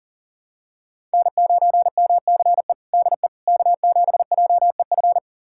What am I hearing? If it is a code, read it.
Morse code, heard as a keyed tone, about 40 wpm, 700 Hz. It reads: N9MCE DE K7JEF
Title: USA Callsign DE Callsign - 5 Characters - 40wpm